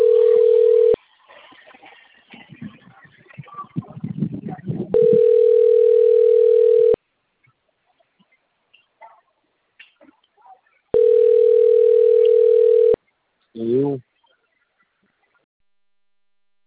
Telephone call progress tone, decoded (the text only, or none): ringback tone